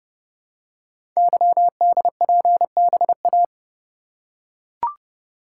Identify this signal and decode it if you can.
Morse code, heard as a keyed tone, about 30 wpm, 700 Hz. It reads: YDPBA
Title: Code Groups - 5 Letters - 30wpm